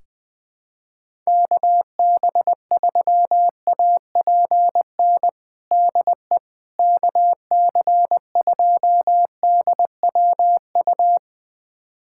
Morse code, heard as a keyed tone, about 21 wpm, 700 Hz. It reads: KB3APN DE KC2DWU
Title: USA Callsign DE Callsign - 6 Characters - 20wpm